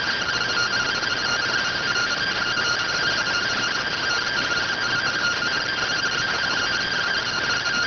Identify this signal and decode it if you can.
Two-tone teleprinter signal, about 170 Hz shift.